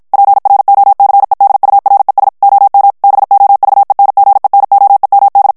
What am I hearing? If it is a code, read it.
Morse code, heard as a keyed tone, about 55 wpm, 800 Hz. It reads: PROCEDURES OM BOVENGENOEMD
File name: HOLLANDAIS texte clair 2 vitesse 50 mots minute